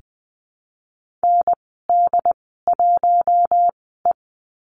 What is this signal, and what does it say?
Morse code, heard as a keyed tone, about 20 wpm, spaced as for 15 wpm, 700 Hz. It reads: ND1E
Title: 076 Lesson 13g - Letter L - Call Signs 20wpm